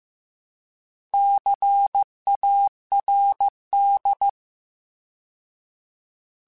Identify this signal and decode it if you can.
Morse code, heard as a keyed tone, about 15 wpm, 800 Hz. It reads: CARD
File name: Alphabetical / Alphabetical mp3s / Cs @13